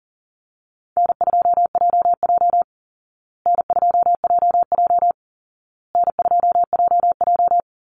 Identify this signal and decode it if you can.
Morse code, heard as a keyed tone, about 40 wpm, 700 Hz. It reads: N2JJ N2JJ N2JJ
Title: Easy Call Signs Repeated 3x - 40wpm